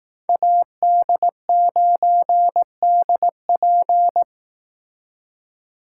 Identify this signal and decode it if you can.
Morse code, heard as a keyed tone, about 19 wpm, 700 Hz. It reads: AD9DP
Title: Speed Racing - USA 2x2 Call Signs 15wpm